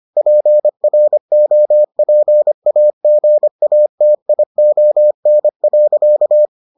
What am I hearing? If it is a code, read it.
Morse code, heard as a keyed tone, about 25 wpm, 600 Hz. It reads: PROPAGATION.